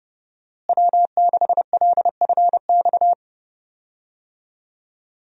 Morse code, heard as a keyed tone, about 32 wpm, 700 Hz. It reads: W6LFX
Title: USA Call Signs - 5 Characters - Repeated 2x - 30wpm